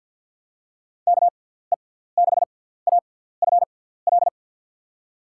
Morse code, heard as a keyed tone, about 51 wpm, spaced as for 17 wpm, 700 Hz. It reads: KE6AFL
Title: USA Call Signs - 6 Characters - 6x Character Spacing - 50wpm